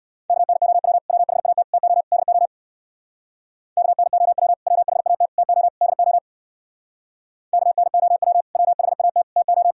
Morse code, heard as a keyed tone, about 61 wpm, 700 Hz. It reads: KNOW WHAT TO DO KNOW WHAT TO DO KNOW WHAT TO
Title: Speed-Racing - Sets of 4 Words 50wpm